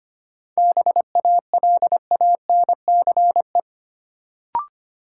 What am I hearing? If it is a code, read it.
Morse code, heard as a keyed tone, about 25 wpm, 700 Hz. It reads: BALANCE
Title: Sentences from Top 900 Words - New Word Drill - 25wpm